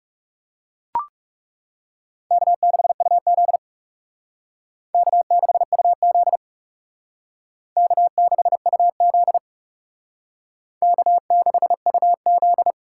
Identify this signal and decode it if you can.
Morse code, heard as a keyed tone, about 39 wpm, 700 Hz. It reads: K6UZ K6UZ K6UZ K6UZ